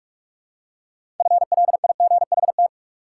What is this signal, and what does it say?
Morse code, heard as a keyed tone, about 44 wpm, 700 Hz. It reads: FLIGHT
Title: Six Letter Words - 45wpm